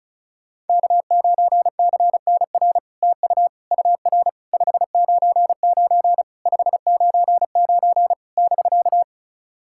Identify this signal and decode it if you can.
Morse code, heard as a keyed tone, about 36 wpm, 700 Hz. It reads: K9CNR TU UR 599 599 <BK>
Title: SOTA QSO Practice 35wpm